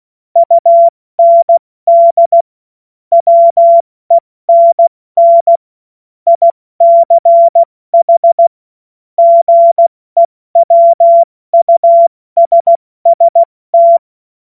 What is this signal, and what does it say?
Morse code, heard as a keyed tone, about 16 wpm, spaced as for 14 wpm, 680 Hz. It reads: UND WENN ICH GEWUSST
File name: Morsekurs / Witze / Witz 05 März 2024 70 F80